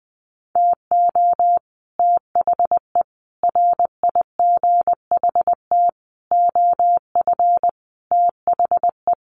Morse code, heard as a keyed tone, about 20 wpm, 700 Hz. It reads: TO THE RIGHT OF THE
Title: Sets of 5 Words Encore 20wpm